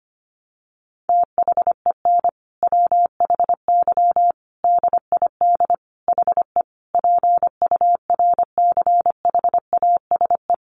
Morse code, heard as a keyed tone, about 25 wpm, 700 Hz. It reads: THEN WHY DID HE PURCHASE
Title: Sentences from Top 800 Words 25wpm